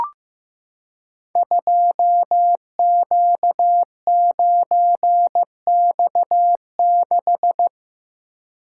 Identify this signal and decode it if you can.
Morse code, heard as a keyed tone, about 15 wpm, 700 Hz. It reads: 2Q9X6
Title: Code Groups - 5 Letter:Numbers - 15wpm